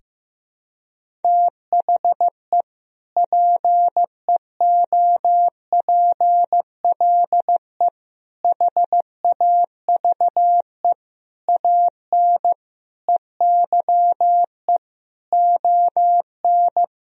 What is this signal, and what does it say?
Morse code, heard as a keyed tone, about 15 wpm, 700 Hz. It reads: THE PEOPLE HAVE AN EYE ON
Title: Sentences from Top 1000 Words - Review Part 1 - 15wpm